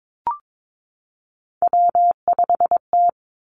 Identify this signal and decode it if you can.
Morse code, heard as a keyed tone, about 22 wpm, 700 Hz. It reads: W5T
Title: Three Challenging Letter-Number Combinations 22wpm